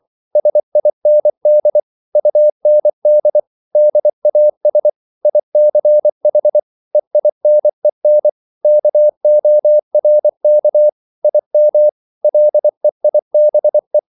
Morse code, heard as a keyed tone, about 24 wpm, 600 Hz. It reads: SIND UND DAS ICH EINEN KORK IM LEIBE